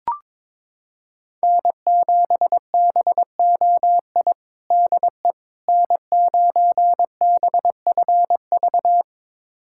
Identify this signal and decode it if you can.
Morse code, heard as a keyed tone, about 22 wpm, 700 Hz. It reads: N7BOI DE N9BFV
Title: USA Callsign DE Callsign - 5 Characters - 22wpm